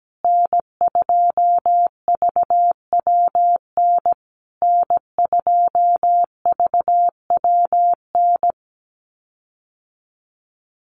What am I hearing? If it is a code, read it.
Morse code, heard as a keyed tone, about 17 wpm, 700 Hz. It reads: N2VWN N2VWN
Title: USA Call Signs Repeated 3x 17wpm